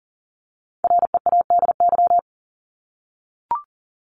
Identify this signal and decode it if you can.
Morse code, heard as a keyed tone, about 40 wpm, 700 Hz. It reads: READY
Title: Top 500 Words 40wpm